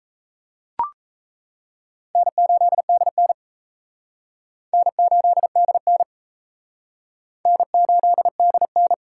Morse code, heard as a keyed tone, about 37 wpm, 700 Hz. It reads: N8DN N8DN N8DN